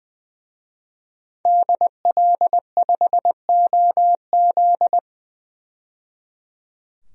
Morse code, heard as a keyed tone, about 20 wpm, 700 Hz. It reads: DL5OZ
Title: DXCC-100 Call Signs - 5 Characters - 20wpm